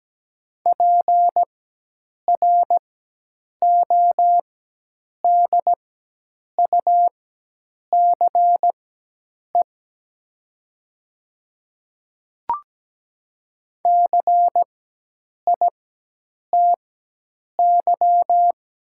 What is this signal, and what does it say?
Morse code, heard as a keyed tone, about 17 wpm, spaced as for 8 wpm, 700 Hz. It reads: PRODUCE CITY
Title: 4x Character Spacing - Top 500 Words - 17wpm